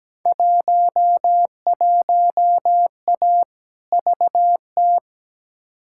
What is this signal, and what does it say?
Morse code, heard as a keyed tone, about 17 wpm, 700 Hz. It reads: 11A VT
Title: Field Day Exchanges Encore - 17wpm